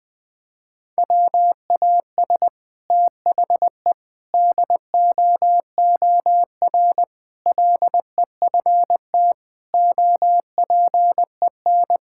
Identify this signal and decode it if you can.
Morse code, heard as a keyed tone, about 21 wpm, 700 Hz. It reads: WAS THE DOOR LEFT OPEN
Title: Sentences from Top 1000 Words - Review Part 6 - 20wpm